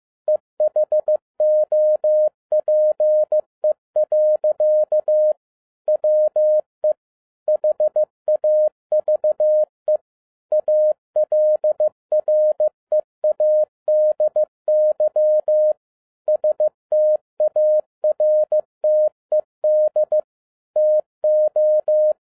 Morse code, heard as a keyed tone, about 15 wpm, 610 Hz. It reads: E HOPE. WE HAVE ALREADY STARTED TO
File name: 2012-01-mp3 / n13-A NEWLY DESIGNED QSL CARD-15wpm